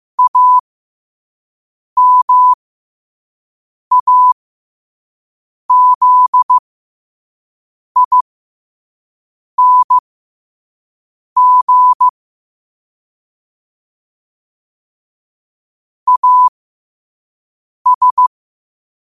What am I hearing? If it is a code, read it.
Morse code, heard as a keyed tone, about 15 wpm, spaced as for 5 wpm, 1000 Hz. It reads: AMAZING AS
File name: qotdfiles / 5wpm